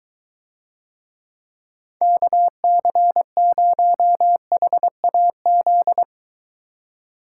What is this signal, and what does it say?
Morse code, heard as a keyed tone, about 23 wpm, 700 Hz. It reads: KC0HAZ